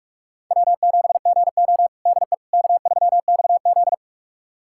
Morse code, heard as a keyed tone, about 46 wpm, 700 Hz. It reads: W7GO DE K3XZ
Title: USA Callsign DE Callsign - 4 Characters - 45wpm